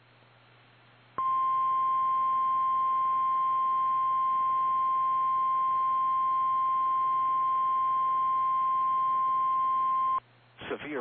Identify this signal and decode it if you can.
Emergency Alert System (SAME) tones (the 1050 Hz attention tone), with speech. It is heard from 1.2 to 10.2 s.
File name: eas / KKPRKXPN / KKPRKXPN